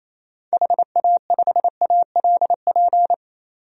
Morse code, heard as a keyed tone, about 27 wpm, 700 Hz. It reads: HA5ALP
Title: DXCC-100 Call Signs - 6 Characters - 28wpm